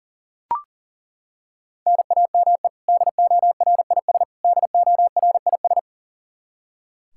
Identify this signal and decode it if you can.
Morse code, heard as a keyed tone, about 40 wpm, 700 Hz. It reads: NAME DORIS DORIS
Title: Practice QSO - Exchange 1 - Beginning - 40wpm